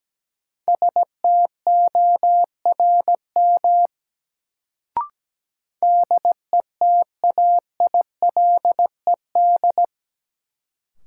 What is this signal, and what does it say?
Morse code, heard as a keyed tone, about 17 wpm, 700 Hz. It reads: STORM DETAILED
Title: Sentences from Top 1500 Words - New Word Drill Alt - 17wpm